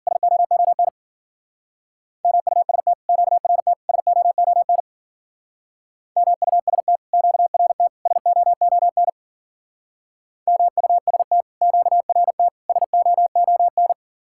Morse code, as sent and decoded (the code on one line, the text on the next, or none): ... --- --- -. / -- ..- ... - / --.- .-. - / ... --- --- -. / -- ..- ... - / --.- .-. - / ... --- --- -. / -- ..- ... - / --.- .-. - / ... --- --- -.
SOON MUST QRT SOON MUST QRT SOON MUST QRT SOON